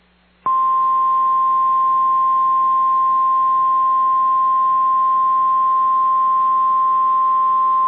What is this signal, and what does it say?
Emergency Alert System (SAME) tones (the 1050 Hz attention tone). It is heard from 0.4 to 7.0 s.